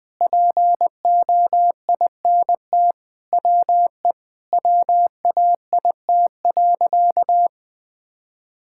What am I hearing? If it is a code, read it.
Morse code, heard as a keyed tone, about 20 wpm, 700 Hz. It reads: POINT WE WAIT.
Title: Sentences with 3-Word Phrases - 20wpm